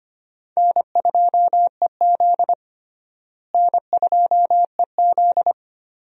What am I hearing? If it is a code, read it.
Morse code, heard as a keyed tone, about 25 wpm, 700 Hz. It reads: N2EZ N2EZ
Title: USA Call Signs - 4 Characters - Repeated 3x - 25wpm